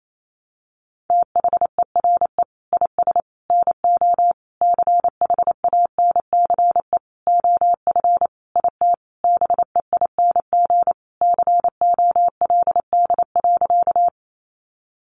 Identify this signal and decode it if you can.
Morse code, heard as a keyed tone, about 28 wpm, 700 Hz. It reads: THERE IS NO CHANCE OF IT BEING COLD.
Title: Sentences from Top 800 Words 28wpm